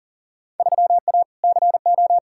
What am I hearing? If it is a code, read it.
Morse code, heard as a keyed tone, about 40 wpm, 700 Hz. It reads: 3A CO
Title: Field Day Exchanges Encore - 40wpm